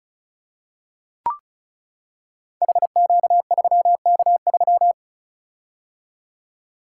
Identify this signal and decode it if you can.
Morse code, heard as a keyed tone, about 36 wpm, 700 Hz. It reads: HQ3K3
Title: Code Groups - 5 Letter:Numbers - 35wpm